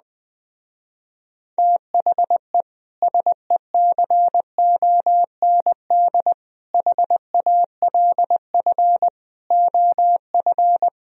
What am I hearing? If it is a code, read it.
Morse code, heard as a keyed tone, about 20 wpm, 700 Hz. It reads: THE SECOND HALF OF